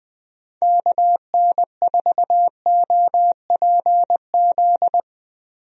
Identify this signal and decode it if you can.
Morse code, heard as a keyed tone, about 21 wpm, 700 Hz. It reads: KN4OPZ
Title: USA Call Signs - 6 Characters - Repeated 2x - 20wpm